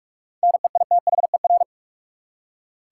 Morse code, heard as a keyed tone, about 46 wpm, 700 Hz. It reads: NEITHER
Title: Seven Letter Words - 45wpm